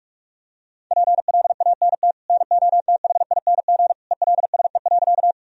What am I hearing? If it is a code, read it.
Morse code, heard as a keyed tone, about 42 wpm, 700 Hz. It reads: PLANT NOTHING ELSE.
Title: Sentences from Top 1000 Words - Review Part 1 - 45wpm